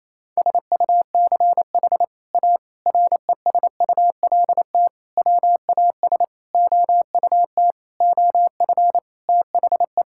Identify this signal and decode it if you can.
Morse code, heard as a keyed tone, about 28 wpm, 700 Hz. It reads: SUCH A RESULT WAS OUT OF THE
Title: Sentences from Top 1000 Words - Review Part 5 - 28wpm